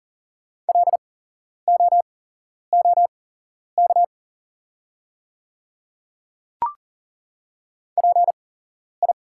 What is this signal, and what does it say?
Morse code, heard as a keyed tone, about 40 wpm, spaced as for 11 wpm, 690 Hz. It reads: LOOK PI
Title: 8x Character Spacing - Top 500 Words - 40wpm